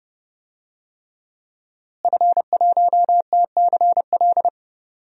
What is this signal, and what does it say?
Morse code, heard as a keyed tone, about 30 wpm, 700 Hz. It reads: F1TCL
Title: DXCC-100 Call Signs - 5 Characters - Repeated 3x - 30wpm